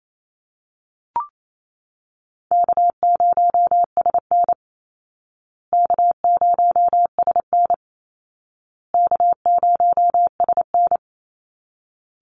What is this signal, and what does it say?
Morse code, heard as a keyed tone, about 28 wpm, 700 Hz. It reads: K0SN K0SN K0SN
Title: Call Signs Repeated 2x - 28wpm